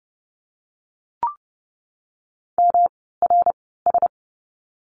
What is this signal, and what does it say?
Morse code, heard as a keyed tone, about 30 wpm, spaced as for 17 wpm, 700 Hz. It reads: MRS
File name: course / 30:10wpm / 108 Lesson 18f - Letter m - Words 30wpm